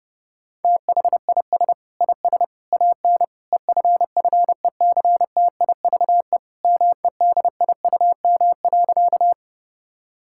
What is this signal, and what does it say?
Morse code, heard as a keyed tone, about 30 wpm, 700 Hz. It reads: THIS IS AN EFFECTIVE MEDIUM.
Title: Sentences from Top 1000 Words - Review Part 6 - 30wpm